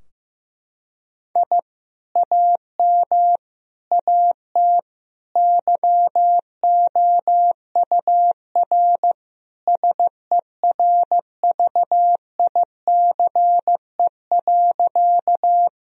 Morse code, heard as a keyed tone, about 15 wpm, 700 Hz. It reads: I AM AT YOUR SERVICE.
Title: Sentences from Top 1000 Words - Review Part 4 - 15wpm